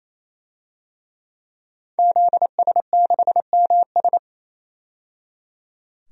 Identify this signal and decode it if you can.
Morse code, heard as a keyed tone, about 28 wpm, 700 Hz. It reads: ZS6MS
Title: DXCC-100 Call Signs - 5 Characters - 28wpm